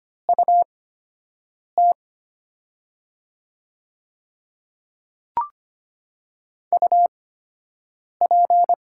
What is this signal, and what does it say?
Morse code, heard as a keyed tone, about 25 wpm, spaced as for 7 wpm, 700 Hz. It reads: UT UP
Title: 8x Character Spacing - Top 500 Words - 25wpm